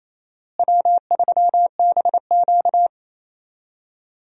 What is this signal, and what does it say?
Morse code, heard as a keyed tone, about 28 wpm, 700 Hz. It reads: W3BQ
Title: USA Call Signs - 4 Characters - Repeated 2x - 28wpm